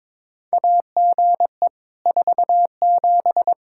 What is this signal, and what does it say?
Morse code, heard as a keyed tone, about 21 wpm, 700 Hz. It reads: AGE 47
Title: Practice QSO - Exchange 3 - More Info - 22wpm